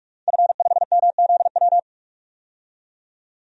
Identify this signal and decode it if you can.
Morse code, heard as a keyed tone, about 41 wpm, 700 Hz. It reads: F5MZW
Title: DXCC-100 Call Signs - 5 Characters - 45wpm